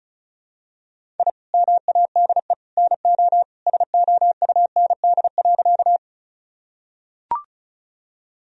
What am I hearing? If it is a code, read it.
Morse code, heard as a keyed tone, about 34 wpm, 700 Hz. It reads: I MADE NO SOUND.
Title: Sentences from Top 1000 Words - Review Part 1 - 35wpm